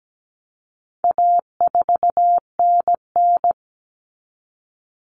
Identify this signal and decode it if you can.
Morse code, heard as a keyed tone, about 17 wpm, 700 Hz. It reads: A4NN
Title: Common Cut Numbers - 4-Digit 17wpm